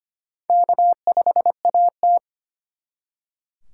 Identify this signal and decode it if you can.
Morse code, heard as a keyed tone, about 32 wpm, 700 Hz. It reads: K5AT